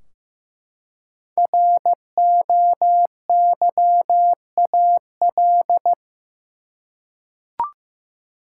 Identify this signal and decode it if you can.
Morse code, heard as a keyed tone, about 15 wpm, 700 Hz. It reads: ROYAL
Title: Five Letter Words - 15wpm